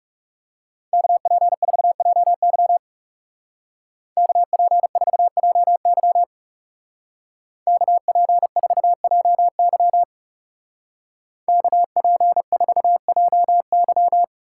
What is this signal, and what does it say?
Morse code, heard as a keyed tone, about 35 wpm, 700 Hz. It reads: KP4JY KP4JY KP4JY KP4JY
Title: Speed Racing - USA Call Signs 30wpm